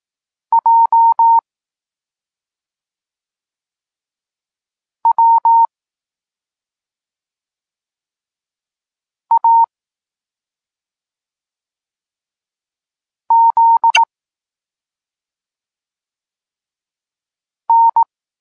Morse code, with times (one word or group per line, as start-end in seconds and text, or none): 0.5-1.4 J
5.1-5.7 W
9.3-9.6 A
13.3-14.0 Z
17.7-18.0 N